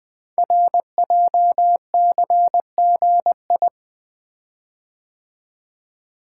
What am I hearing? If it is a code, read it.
Morse code, heard as a keyed tone, about 20 wpm, 700 Hz. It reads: RJCGI
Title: Code Groups - 5 Letters - 20wpm